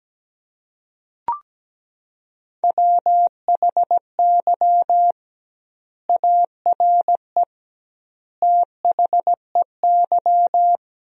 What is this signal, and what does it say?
Morse code, heard as a keyed tone, about 17 wpm, 700 Hz. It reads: WHY ARE THEY
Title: 2x Word-Spacing - Sentences from Top 200 Words - 17wpm